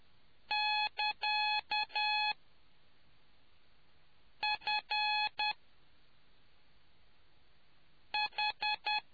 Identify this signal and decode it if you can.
Morse code, heard as a keyed tone, about 10 wpm, spaced as for 3 wpm, 850 Hz. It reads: <KA> FH